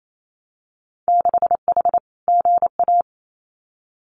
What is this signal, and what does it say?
Morse code, heard as a keyed tone, about 28 wpm, 700 Hz. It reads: 6H GA
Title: Winter Field Day Exchanges Encore 28wpm